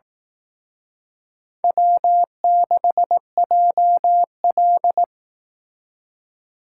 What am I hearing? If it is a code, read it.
Morse code, heard as a keyed tone, about 18 wpm, 700 Hz. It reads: W6JL
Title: Speed Racing - USA 1x2 Call Signs 15wpm